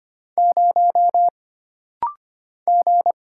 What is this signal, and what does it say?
Morse code, heard as a keyed tone, about 25 wpm, 700 Hz. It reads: 0 G
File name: Single Character (Letter:Number:Punctuation:Prosign) - Rapid-Fire - 25wpm